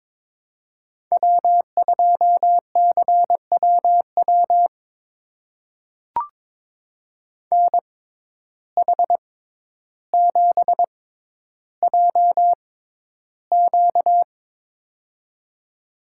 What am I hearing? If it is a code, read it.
Morse code, heard as a keyed tone, about 22 wpm, spaced as for 11 wpm, 700 Hz. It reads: W2CWW NH7JQ
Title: USA Call Signs - 5 Characters - 6x Character Spacing - 22wpm